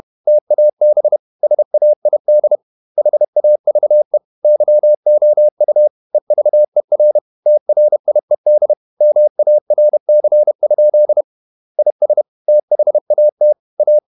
Morse code, heard as a keyed tone, about 31 wpm, 600 Hz. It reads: TAB SAID HAVE YOU EVER TRIED MARC? IS THAT A
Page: Never 31wpm | CW med Gnister